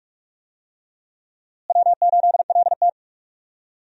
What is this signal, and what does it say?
Morse code, heard as a keyed tone, about 45 wpm, 700 Hz. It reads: W8LT
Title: USA Call Signs - 4 Characters - Repeated 3x - 45wpm